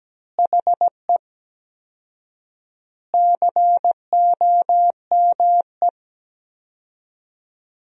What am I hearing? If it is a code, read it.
Morse code, heard as a keyed tone, about 17 wpm, 700 Hz. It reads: HE COME
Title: 4x Word-Spacing - Sentences from Top 200 Words - 17wpm